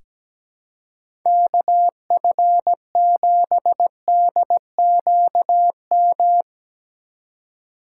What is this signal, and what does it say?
Morse code, heard as a keyed tone, about 17 wpm, 700 Hz. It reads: KF7DQM
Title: USA Call Signs - 2x Character Spacing - 17wpm